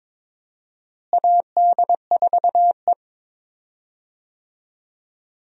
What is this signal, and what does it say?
Morse code, heard as a keyed tone, about 22 wpm, 700 Hz. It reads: AD4E
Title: USA Call Signs - 4 Characters - Repeated 2x - 22wpm